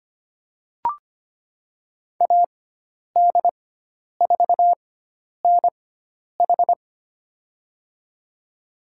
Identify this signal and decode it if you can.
Morse code, heard as a keyed tone, about 25 wpm, spaced as for 10 wpm, 700 Hz. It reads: AD4NH
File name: Lesson 16g - Letter c - Call Signs 25wpm